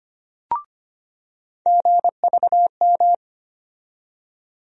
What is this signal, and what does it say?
Morse code, heard as a keyed tone, about 25 wpm, 700 Hz. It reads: GVM
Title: USA Call Sign Postfixes - 3 Letters 25wpm